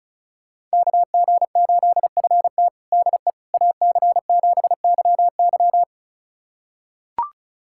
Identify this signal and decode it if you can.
Morse code, heard as a keyed tone, about 34 wpm, 700 Hz. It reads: KG8FT DE AC7YY
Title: USA Callsign DE Callsign - 5 Characters - 35wpm